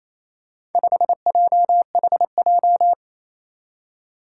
Morse code, heard as a keyed tone, about 28 wpm, 700 Hz. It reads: 5JHJ
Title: Four Challenging Letter-Number Combinations 28wpm